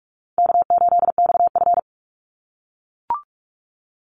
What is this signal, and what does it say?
Morse code, heard as a keyed tone, about 44 wpm, 700 Hz. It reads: K8XF
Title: Call Signs - 45wpm